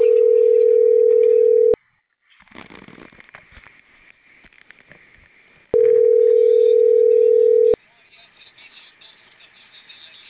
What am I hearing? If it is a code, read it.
Ringback tone (North America), 440 + 480 Hz.